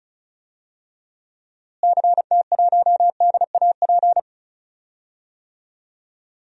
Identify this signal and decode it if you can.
Morse code, heard as a keyed tone, about 34 wpm, 700 Hz. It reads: CT1DAP
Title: DXCC-100 Call Signs - 6 Characters - Repeated 3x - 35wpm